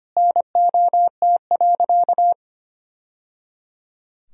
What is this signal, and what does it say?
Morse code, heard as a keyed tone, about 25 wpm, 700 Hz. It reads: NOT.
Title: Sentences from Top 1000 Words - Review Part 4 - 25wpm